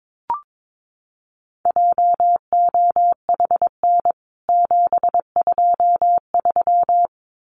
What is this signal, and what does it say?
Morse code, heard as a keyed tone, about 22 wpm, 700 Hz. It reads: JOHN 723
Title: CWT Exchange 22wpm